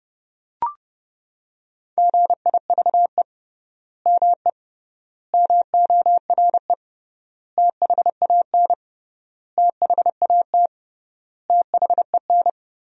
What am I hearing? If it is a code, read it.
Morse code, heard as a keyed tone, about 32 wpm, 700 Hz. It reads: GIVE ME MORE THAN THAT THEN
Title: 3x Word-Spacing - Sentences from Top 200 Words - 30wpm